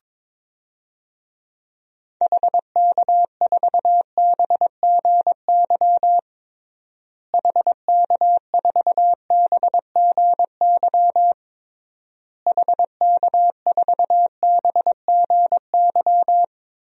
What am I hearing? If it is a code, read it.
Morse code, heard as a keyed tone, about 22 wpm, 700 Hz. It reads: HK4BGY HK4BGY HK4BGY
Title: DXCC-100 Call Signs - 6 Characters - Repeated 3x - 22wpm